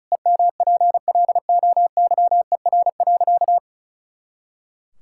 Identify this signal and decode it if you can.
Morse code, heard as a keyed tone, about 36 wpm, 700 Hz. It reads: EMPLOYER.
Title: Pangram Challenge v2 - 35wpm